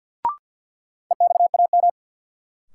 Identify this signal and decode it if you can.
Morse code, heard as a keyed tone, about 48 wpm, 700 Hz. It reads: EXAM
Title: Sentences from Top 1000 Words - Review Part 5 - New Word Drill - 50wpm